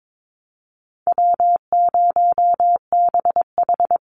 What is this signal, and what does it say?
Morse code, heard as a keyed tone, about 22 wpm, 700 Hz. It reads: W0BH
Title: Easy Call Signs Repeated 2x - 22wpm